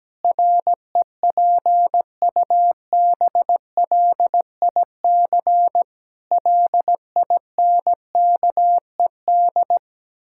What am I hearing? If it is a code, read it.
Morse code, heard as a keyed tone, about 17 wpm, 700 Hz. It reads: REPUBLIC LINKED
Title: Pangram Challenge v2 - 17wpm